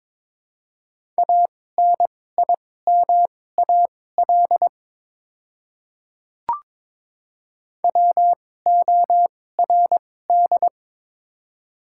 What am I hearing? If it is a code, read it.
Morse code, heard as a keyed tone, about 22 wpm, spaced as for 16 wpm, 700 Hz. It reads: ANIMAL WORD
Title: 2x Character Spacing - Top 200 Words - 22wpm